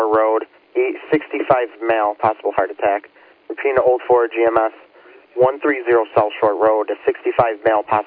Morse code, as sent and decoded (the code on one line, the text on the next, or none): none